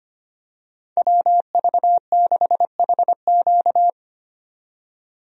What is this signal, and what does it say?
Morse code, heard as a keyed tone, about 25 wpm, 700 Hz. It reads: WV6HQ
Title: USA Call Signs - 5 Characters - 25wpm